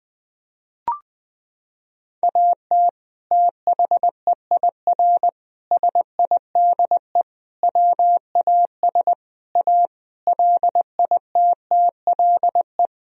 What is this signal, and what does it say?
Morse code, heard as a keyed tone, about 20 wpm, 700 Hz. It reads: AT THEIR SIDE WAS A LITTLE
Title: Sentences from Top 1000 Words - Review Part 2 - 20wpm